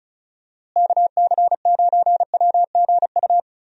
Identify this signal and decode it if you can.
Morse code, heard as a keyed tone, about 34 wpm, 700 Hz. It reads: KC9WGU
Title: USA Call Signs - 6 Characters - Repeated 3x - 35wpm